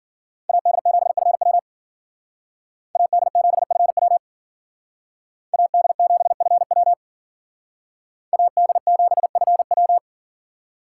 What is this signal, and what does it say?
Morse code, heard as a keyed tone, about 49 wpm, 700 Hz. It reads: AD7FW AD7FW AD7FW AD7FW
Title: Speed Racing - USA Call Signs 40wpm